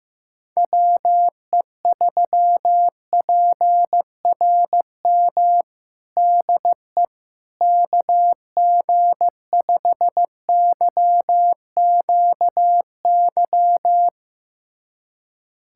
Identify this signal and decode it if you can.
Morse code, heard as a keyed tone, about 15 wpm, 700 Hz. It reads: WE3PRM DE KG5YQY
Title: USA Callsign DE Callsign - 6 Characters - 15wpm